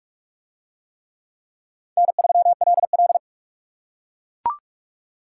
Morse code, heard as a keyed tone, about 44 wpm, 700 Hz. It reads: N3LL